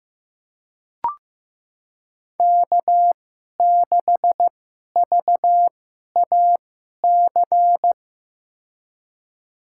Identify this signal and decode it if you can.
Morse code, heard as a keyed tone, about 15 wpm, spaced as for 11 wpm, 700 Hz. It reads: K6VAC
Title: USA Call Signs - 2x Character Spacing - 15wpm